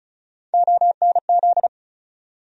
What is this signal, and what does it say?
Morse code, heard as a keyed tone, about 36 wpm, 700 Hz. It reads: ONZ
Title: USA Call Sign Postfixes - 3 Letters 35wpm